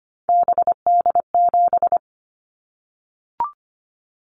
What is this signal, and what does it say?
Morse code, heard as a keyed tone, about 25 wpm, 700 Hz. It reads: BD7
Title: Callsign Prefixes for Top 100 DXCC - 25wpm